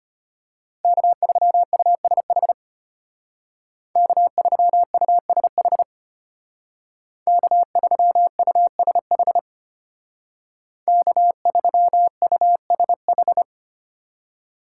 Morse code, heard as a keyed tone, about 33 wpm, 700 Hz. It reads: K3USH K3USH K3USH K3USH